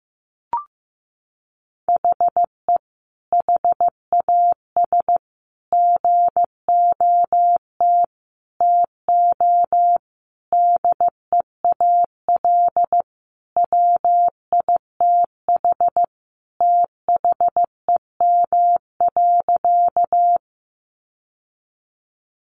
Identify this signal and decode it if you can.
Morse code, heard as a keyed tone, about 15 wpm, 700 Hz. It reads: HE HAS GOT TO DEAL WITH THEM.
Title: Sentences from Top 800 Words 15wpm